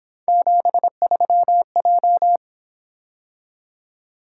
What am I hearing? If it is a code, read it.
Morse code, heard as a keyed tone, about 26 wpm, 700 Hz. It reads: 73J
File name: Speed-Racing - Three Letter-Number Combinations - 22wpm